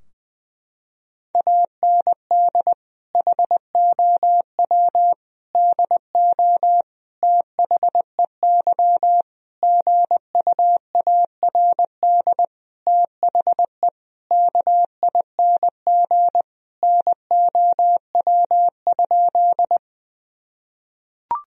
Morse code, heard as a keyed tone, about 20 wpm, 700 Hz. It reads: AND HOW DO THEY GUARD THE KING NOW?
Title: Sentences from Top 1000 Words - Review Part 4 - 20wpm